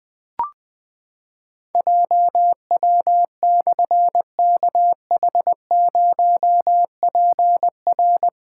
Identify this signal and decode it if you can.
Morse code, heard as a keyed tone, about 20 wpm, 700 Hz. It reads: JW/KH0PR
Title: Difficult Call Signs 20wpm